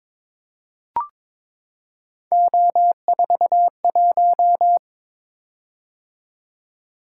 Morse code, heard as a keyed tone, about 23 wpm, 700 Hz. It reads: O41
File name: practice / 22wpm / Three Letter-Number Combinations - 22wpm